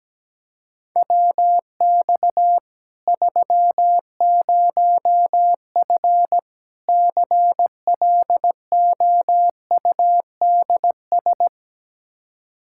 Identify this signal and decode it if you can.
Morse code, heard as a keyed tone, about 17 wpm, 700 Hz. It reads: WX 30F CLOUDS
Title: Practice QSO - Exchange 2 - Info - 17wpm